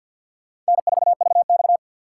Morse code, heard as a keyed tone, about 62 wpm, 700 Hz. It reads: N4VX
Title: Speed Racing - USA Call Signs 50wpm